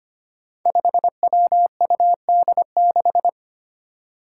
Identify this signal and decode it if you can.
Morse code, heard as a keyed tone, about 25 wpm, 700 Hz. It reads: 5WUD6
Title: Five Challenging Letter-Number Combinations 25wpm